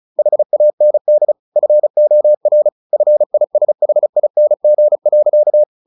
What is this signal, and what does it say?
Morse code, heard as a keyed tone, about 34 wpm, 600 Hz. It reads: HAND FOR FISHING.
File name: the old man and the sea 2 35wpm